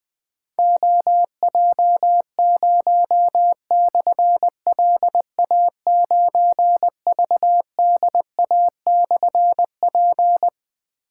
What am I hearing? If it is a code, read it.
Morse code, heard as a keyed tone, about 21 wpm, 700 Hz. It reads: OJ0/LA9VDA/P
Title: Difficult Call Signs 20wpm